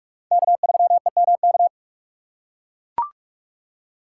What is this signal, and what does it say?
Morse code, heard as a keyed tone, about 43 wpm, 700 Hz. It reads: K3EMK
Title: USA Call Signs - 5 Characters - 4x Character Spacing - 45wpm